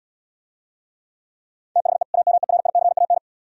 Morse code, heard as a keyed tone, about 76 wpm, 700 Hz. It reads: THE AMERICAN